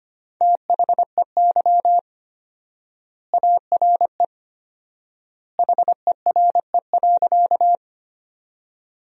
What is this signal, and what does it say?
Morse code, heard as a keyed tone, about 25 wpm, 700 Hz. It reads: THEY ARE HERE.
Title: 4x Word-Spacing - Sentences from Top 200 Words - 25wpm